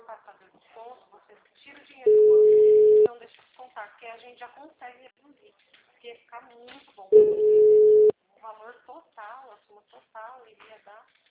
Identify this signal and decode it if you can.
Ringback tone (Europe, 425 Hz), with speech.